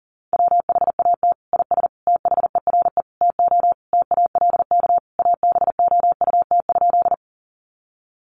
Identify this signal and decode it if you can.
Morse code, heard as a keyed tone, about 40 wpm, 700 Hz. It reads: WHAT IS THERE TO TALK ABOUT?
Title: Sentences from Top 400 Words 40wpm